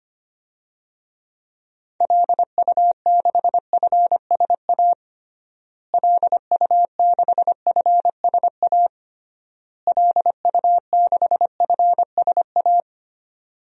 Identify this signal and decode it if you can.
Morse code, heard as a keyed tone, about 25 wpm, 700 Hz. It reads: LU6FSA LU6FSA LU6FSA
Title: DXCC-100 Call Signs - 6 Characters - Repeated 3x - 25wpm